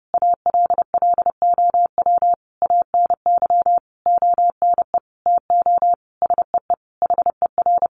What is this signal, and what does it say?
Morse code, heard as a keyed tone, about 30 wpm, 700 Hz. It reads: ALLOW ANY ONE TO SEE HER
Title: Sentences from Top 1200 Words 30wpm